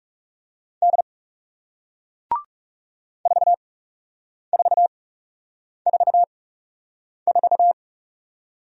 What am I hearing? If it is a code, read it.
Morse code, heard as a keyed tone, about 41 wpm, spaced as for 8 wpm, 700 Hz. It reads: D 4444